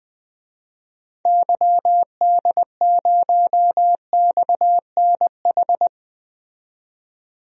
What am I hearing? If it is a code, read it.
Morse code, heard as a keyed tone, about 20 wpm, 700 Hz. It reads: YD0XNH
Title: DXCC-100 Call Signs - 6 Characters - 20wpm